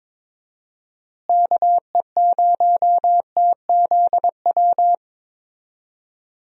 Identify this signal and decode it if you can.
Morse code, heard as a keyed tone, about 22 wpm, 700 Hz. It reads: KE0TZW
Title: USA Call Signs - 6 Characters - Repeated 3x - 22wpm